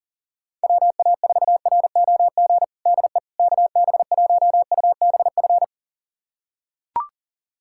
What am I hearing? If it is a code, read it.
Morse code, heard as a keyed tone, about 40 wpm, 700 Hz. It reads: WA4ROG DE KB1UBF
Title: USA Callsign DE Callsign - 6 Characters - 40wpm